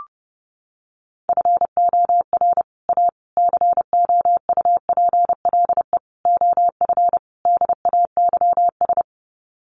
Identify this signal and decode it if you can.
Morse code, heard as a keyed tone, about 30 wpm, 700 Hz. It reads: FOR A COUPLE OF DAYS
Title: Sets of 5 Words Encore2 30wpm